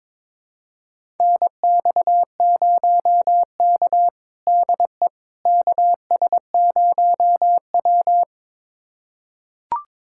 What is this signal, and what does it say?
Morse code, heard as a keyed tone, about 22 wpm, 700 Hz. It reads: NX0K DE KS0W
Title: USA Callsign DE Callsign - 4 Characters - 22wpm